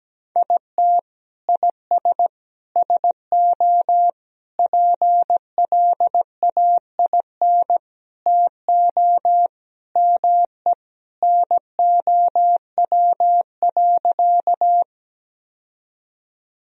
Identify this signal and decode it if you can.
Morse code, heard as a keyed tone, about 17 wpm, 700 Hz. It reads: IT IS SO PLAIN TO ME NOW.
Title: Sentences from Top 1000 Words - Review Part 2 - 17wpm